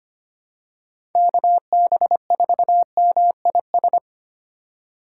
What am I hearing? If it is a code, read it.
Morse code, heard as a keyed tone, about 25 wpm, 700 Hz. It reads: KB4MIS
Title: USA Call Signs - 3x Character Spacing - 25wpm